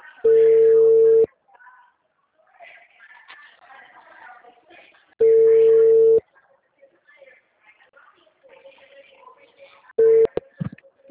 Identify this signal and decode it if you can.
Ringback tone (China, 450 Hz), with speech.
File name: out- -2012- - - .67766